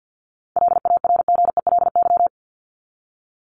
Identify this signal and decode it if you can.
Morse code, heard as a keyed tone, about 50 wpm, 700 Hz. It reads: LARGELY
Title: LY Suffix - Top 100 Words with Base Words 50wpm